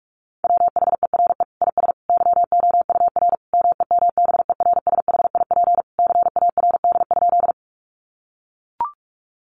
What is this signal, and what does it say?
Morse code, heard as a keyed tone, about 44 wpm, 700 Hz. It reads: WHERE IS YOUR MEMBERSHIP CARD?
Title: Sentences from Top 1400 Words 45wpm